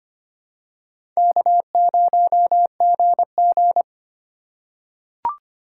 Morse code, heard as a keyed tone, about 25 wpm, 700 Hz. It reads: K0GG
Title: USA Call Signs - 4 Characters - 4x Character Spacing - 25wpm